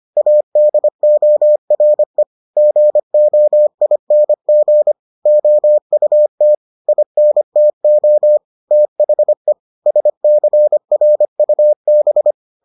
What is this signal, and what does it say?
Morse code, heard as a keyed tone, about 25 wpm, 600 Hz. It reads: ADORE GOING OUT INTO THE SCRUB